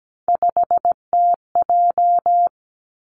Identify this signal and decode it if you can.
Morse code, heard as a keyed tone, about 17 wpm, 700 Hz. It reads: 5TJ
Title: Three Challenging Letter-Number Combinations 17wpm